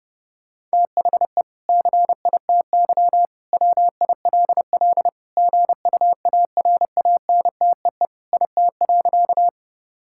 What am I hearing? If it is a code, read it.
Morse code, heard as a keyed tone, about 30 wpm, 700 Hz. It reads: THE CITY WILL GUARANTEE IT.
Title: Sentences from Top 1600 Words 30wpm